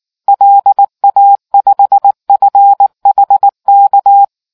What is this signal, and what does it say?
Morse code, heard as a keyed tone, about 19 wpm, 800 Hz. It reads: LA5FHK